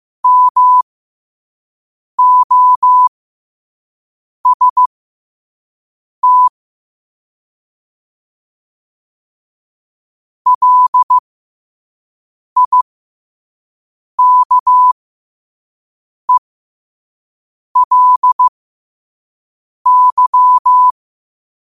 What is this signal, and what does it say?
Morse code, heard as a keyed tone, about 15 wpm, spaced as for 5 wpm, 1000 Hz. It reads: MOST LIKELY
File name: qotdfiles / 5wpm